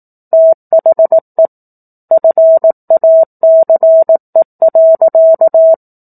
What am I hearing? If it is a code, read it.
Morse code, heard as a keyed tone, about 18 wpm, 650 Hz. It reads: THE FACE.
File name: 04-Chapter-04 18 WPM